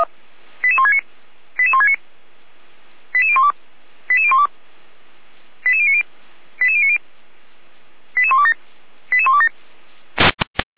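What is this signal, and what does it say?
Five-tone selective call (ZVEI), heard as quick runs of five tones, tones of about 70 ms. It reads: 88169 88169 88012 88012 88090 88090 88127 88127
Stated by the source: ZVEI five-tone call received by SDR, from the Feuerwehr Orlamünde (fire brigade) server audio.f85360.feuerwehr-orlamuende.org.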